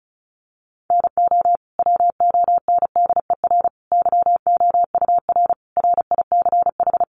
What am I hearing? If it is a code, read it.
Morse code, heard as a keyed tone, about 34 wpm, 700 Hz. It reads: NO WONDER YOUR RICH
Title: Sentences from Top 1100 Words 35wpm